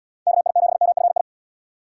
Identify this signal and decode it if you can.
Morse code, heard as a keyed tone, about 70 wpm, 700 Hz. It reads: KI7MFI